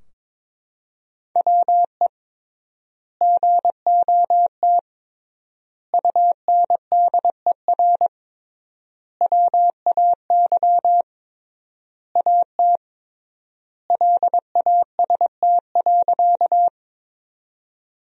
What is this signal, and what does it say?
Morse code, heard as a keyed tone, about 21 wpm, 700 Hz. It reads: WE GOT UNDER WAY AT LAST.
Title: 3x Word-Spacing - Sentences from Top 300 Words - 22wpm